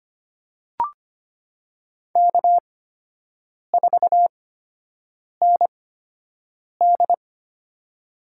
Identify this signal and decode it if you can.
Morse code, heard as a keyed tone, about 25 wpm, spaced as for 7 wpm, 700 Hz. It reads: K4ND
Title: USA Call Signs - 8x Character Spacing - 25wpm